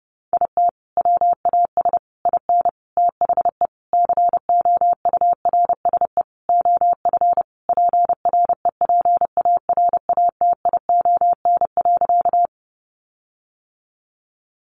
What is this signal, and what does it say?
Morse code, heard as a keyed tone, about 30 wpm, 700 Hz. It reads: IT WAS IN THE COURSE OF PREPARATION.
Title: Sentences from Top 1400 Words 30wpm